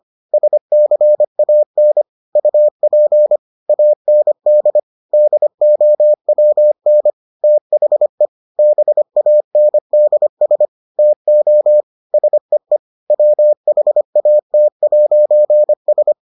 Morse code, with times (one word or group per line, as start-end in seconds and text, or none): 0.3-2.0 SCAN
2.4-3.4 UP
3.7-4.8 AND
5.1-7.1 DOWN
7.4-8.3 THE
8.6-10.7 BANDS
11.0-11.8 TO
12.1-12.8 SEE
13.1-16.1 WHAT'S